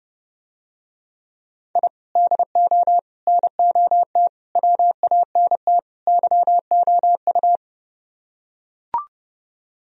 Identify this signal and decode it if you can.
Morse code, heard as a keyed tone, about 32 wpm, 700 Hz. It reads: I DO NOT WANT YOU